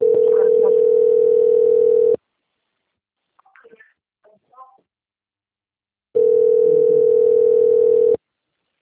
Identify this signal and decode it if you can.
Ringback tone (North America), 440 + 480 Hz.
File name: out- - - - - .23916